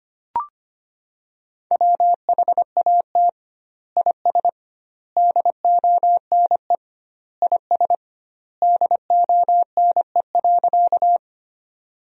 Morse code, heard as a keyed tone, about 25 wpm, 700 Hz. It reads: WHAT IS DONE IS DONE.
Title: 2x Word-Spacing - Sentences from Top 500 Words - 25wpm